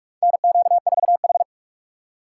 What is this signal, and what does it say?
Morse code, heard as a keyed tone, about 46 wpm, 700 Hz. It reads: NQ4H
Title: USA Call Signs - 4 Characters - 4x Character Spacing - 45wpm